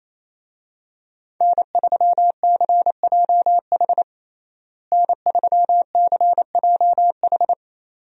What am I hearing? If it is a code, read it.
Morse code, heard as a keyed tone, about 28 wpm, 700 Hz. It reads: N3CJH N3CJH
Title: USA Call Signs - 5 Characters - Repeated 2x - 28wpm